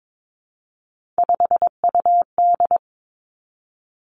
Morse code, heard as a keyed tone, about 22 wpm, 700 Hz. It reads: 5UD
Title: Three Challenging Letter-Number Combinations 22wpm